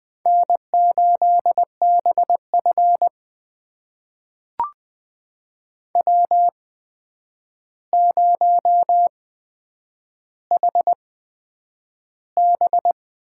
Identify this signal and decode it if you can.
Morse code, heard as a keyed tone, about 20 wpm, spaced as for 9 wpm, 700 Hz. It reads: N8BF W0HB
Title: USA Call Signs - 4 Characters - 8x Character Spacing - 20wpm